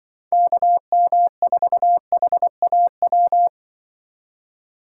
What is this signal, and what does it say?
Morse code, heard as a keyed tone, about 24 wpm, 700 Hz. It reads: KM4HAW
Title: Speed Racing - USA Call Signs 20wpm